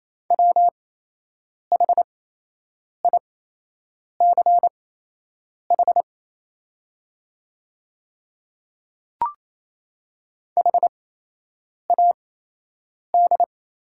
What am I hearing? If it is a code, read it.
Morse code, heard as a keyed tone, about 28 wpm, spaced as for 8 wpm, 700 Hz. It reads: WHICH HAD
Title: 8x Character Spacing - Top 100 Words - 28wpm